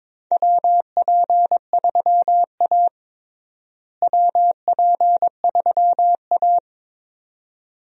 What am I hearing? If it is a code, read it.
Morse code, heard as a keyed tone, about 22 wpm, 700 Hz. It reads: WP3A WP3A
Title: USA Call Signs - 4 Characters - Repeated 2x - 22wpm